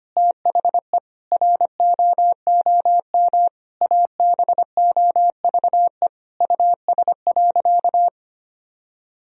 Morse code, heard as a keyed tone, about 25 wpm, 700 Hz. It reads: THE ROOM ABOVE US.
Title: Sentences from Top 1000 Words - Review Part 3 - 25wpm